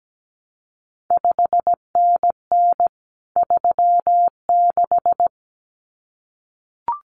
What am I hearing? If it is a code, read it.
Morse code, heard as a keyed tone, about 17 wpm, 700 Hz. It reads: 5NN 36
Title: CQ WW DX Contest Exchange Encore - 17wpm